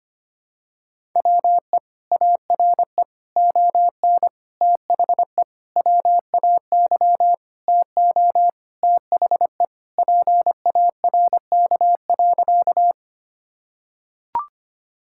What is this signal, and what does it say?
Morse code, heard as a keyed tone, about 25 wpm, 700 Hz. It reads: WE ARE ON THE WAY TO THE PARK.
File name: practice / 25wpm / Sentences with 4-Word Phrases - 25wpm